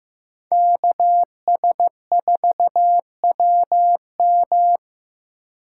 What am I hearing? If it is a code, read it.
Morse code, heard as a keyed tone, about 15 wpm, 700 Hz. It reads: KS4WM
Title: USA Call Signs - 8x Character Spacing - 15wpm